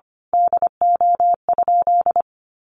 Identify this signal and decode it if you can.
Morse code, heard as a keyed tone, about 25 wpm, 700 Hz. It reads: DO?
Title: Sentences from Top 1100 Words 25wpm